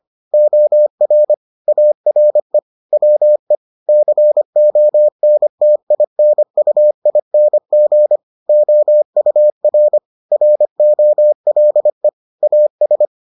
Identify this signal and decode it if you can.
Morse code, heard as a keyed tone, about 25 wpm, 600 Hz. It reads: OR ARE WE CONTINUING OUR ROLE AS